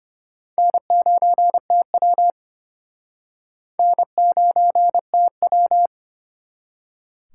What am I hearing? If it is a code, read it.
Morse code, heard as a keyed tone, about 31 wpm, 700 Hz. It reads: N9TW N9TW
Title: Speed Racing - USA 1x2 Call Signs 25wpm